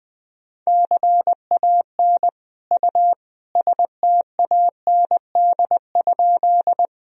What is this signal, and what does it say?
Morse code, heard as a keyed tone, about 21 wpm, 700 Hz. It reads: CAN U STAND?
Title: Sentences from Top 300 Words and Abbreviations 20wpm